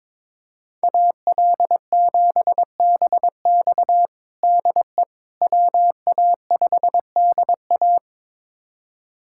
Morse code, heard as a keyed tone, about 22 wpm, 700 Hz. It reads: AL7BX DE WA5DA
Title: USA Callsign DE Callsign - 5 Characters - 22wpm